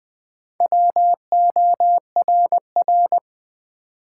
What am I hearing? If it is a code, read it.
Morse code, heard as a keyed tone, about 20 wpm, 700 Hz. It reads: WORR
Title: Build A Word - 1 Letter At A Time - 3 to 5-Letter Words - 20wpm